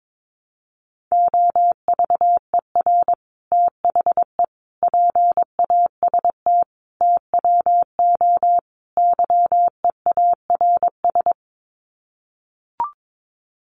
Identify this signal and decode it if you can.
Morse code, heard as a keyed tone, about 22 wpm, 700 Hz. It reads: OVER THE PAST TWO YEARS
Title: Sets of 5 Words Encore 22wpm